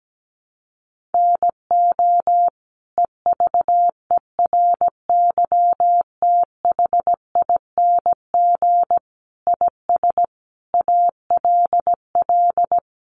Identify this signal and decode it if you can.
Morse code, heard as a keyed tone, about 17 wpm, 700 Hz. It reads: NO EVERYTHING IS ALL
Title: Sentences from Top 700 Words 17wpm